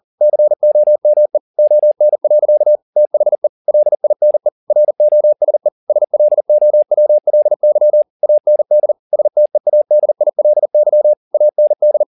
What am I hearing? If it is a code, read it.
Morse code, heard as a keyed tone, about 40 wpm, 600 Hz. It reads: COME ON. THE LINE ROSE SLOWLY AND STEADILY AND